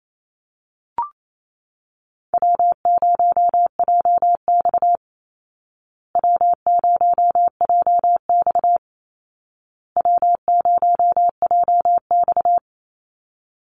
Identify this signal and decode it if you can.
Morse code, heard as a keyed tone, about 28 wpm, 700 Hz. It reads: W0JX W0JX W0JX
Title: Easy Call Signs Repeated 3x - 28wpm